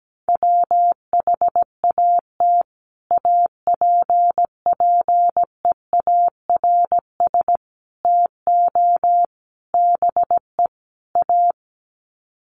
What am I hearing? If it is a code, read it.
Morse code, heard as a keyed tone, about 17 wpm, 700 Hz. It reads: WHAT APPEARS TO BE A
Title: Sets of 5 Words Encore2 17wpm